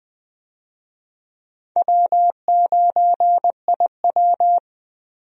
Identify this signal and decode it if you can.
Morse code, heard as a keyed tone, about 20 wpm, 700 Hz. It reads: W9IW
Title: USA Call Signs - 4 Characters - Repeated 2x - 20wpm